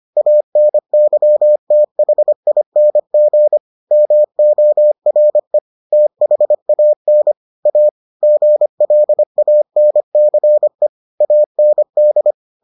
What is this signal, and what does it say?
Morse code, heard as a keyed tone, about 25 wpm, 600 Hz. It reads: ANYTHING MORE THAN A GLANCE AND